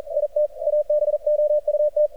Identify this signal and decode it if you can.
Morse code, heard as a keyed tone, about 39 wpm, 600 Hz. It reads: IT ABOUT
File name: archived / CRSnet